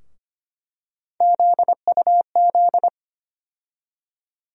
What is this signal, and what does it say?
Morse code, heard as a keyed tone, about 25 wpm, 700 Hz. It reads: ZUZ
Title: USA Call Sign Postfixes - 3 Letters 25wpm